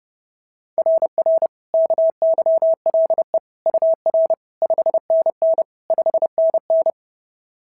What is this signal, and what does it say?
Morse code, heard as a keyed tone, about 30 wpm, 690 Hz. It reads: RR KYLE UR 5NN 5NN
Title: SOTA QSO Practice 30wpm